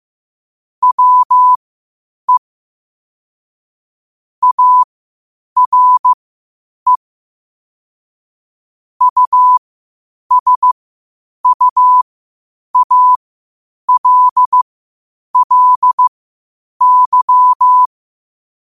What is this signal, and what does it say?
Morse code, heard as a keyed tone, about 15 wpm, spaced as for 8 wpm, 1000 Hz. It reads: WE ARE USUALLY
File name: qotdfiles / 8wpm